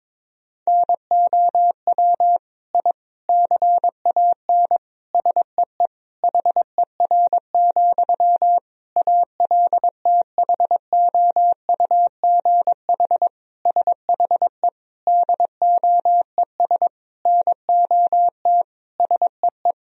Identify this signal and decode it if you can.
Morse code, heard as a keyed tone, about 22 wpm, 700 Hz. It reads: NOW I CAN SEE HER, ALTHOUGH SHE DOES NOT SEE
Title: Sentences from Top 1000 Words - Review Part 3 - 22wpm